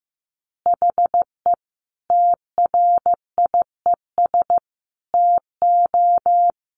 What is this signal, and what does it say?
Morse code, heard as a keyed tone, about 15 wpm, 700 Hz. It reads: HE TRIES TO
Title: Easy Phrases 15wpm